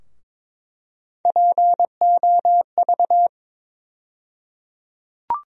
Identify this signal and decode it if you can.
Morse code, heard as a keyed tone, about 22 wpm, 700 Hz. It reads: POV
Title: USA Call Sign Postfixes - 3 Letters 22wpm